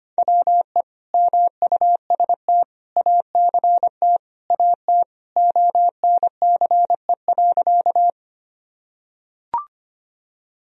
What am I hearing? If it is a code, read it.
Morse code, heard as a keyed tone, about 25 wpm, 700 Hz. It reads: WE MUST ACT AT ONCE.
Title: Sentences from Top 1000 Words - Review Part 1 - 25wpm